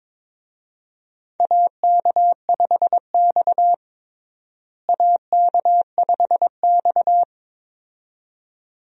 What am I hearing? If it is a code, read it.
Morse code, heard as a keyed tone, about 22 wpm, 700 Hz. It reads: AK5X AK5X
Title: USA Call Signs - 4 Characters - Repeated 2x - 22wpm